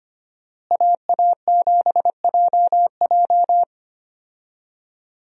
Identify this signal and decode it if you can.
Morse code, heard as a keyed tone, about 25 wpm, 700 Hz. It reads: AA7JJ
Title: USA Call Signs - 5 Characters - Repeated 3x - 25wpm